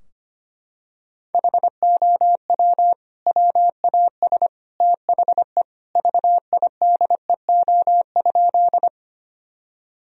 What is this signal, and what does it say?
Morse code, heard as a keyed tone, about 25 wpm, 700 Hz. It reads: HOW WAS THE VIDEO?
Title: Sentences from Top 1000 Words - Review Part 4 - 25wpm